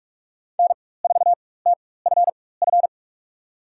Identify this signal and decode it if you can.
Morse code, heard as a keyed tone, about 46 wpm, spaced as for 21 wpm, 690 Hz. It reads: N4TFF
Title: USA Call Signs - 4x Character Spacing - 45wpm